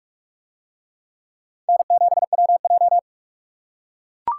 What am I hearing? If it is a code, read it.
Morse code, heard as a keyed tone, about 44 wpm, 700 Hz. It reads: N7WJ